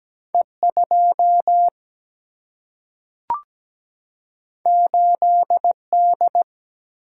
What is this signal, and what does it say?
Morse code, heard as a keyed tone, about 17 wpm, 700 Hz. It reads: E2 8D
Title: Single-Letter Single-Number 17wpm